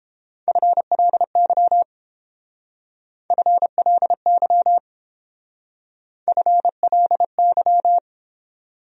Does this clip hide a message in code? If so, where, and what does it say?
Yes: Morse code from 0.5 to 8.0 s, heard as a keyed tone, about 28 wpm, 700 Hz. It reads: FLY FLY FLY